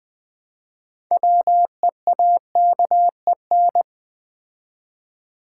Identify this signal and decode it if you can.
Morse code, heard as a keyed tone, about 20 wpm, 690 Hz. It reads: WEAKEN
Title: Build A Word - 2 Letters At A Time - 6-Letter Words - 20wpm